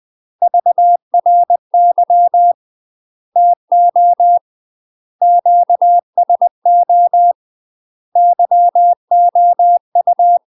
Morse code, heard as a keyed tone, about 20 wpm, 700 Hz. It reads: VRY TO QSO YOU
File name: UK MASTER QSOs TRAINING / 20WPM / UK MASTER QSOs 0053